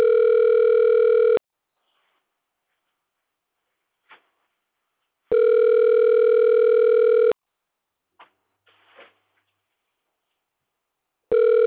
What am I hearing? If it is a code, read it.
Ringback tone (North America), 440 + 480 Hz.